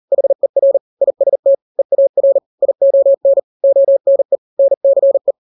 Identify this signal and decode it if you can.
Morse code, heard as a keyed tone, about 40 wpm, 550 Hz. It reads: HER IST EAR ION ONE NCE
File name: n-grams / 3-grams 40 35 WPM